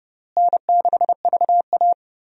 Morse code, heard as a keyed tone, about 29 wpm, 700 Hz. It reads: N6VA
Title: USA Call Signs - 4 Characters - Repeated 3x - 30wpm